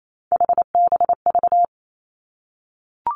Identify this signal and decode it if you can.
Morse code, heard as a keyed tone, about 28 wpm, 700 Hz. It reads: HBV
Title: Three Challenging Letter-Number Combinations 28wpm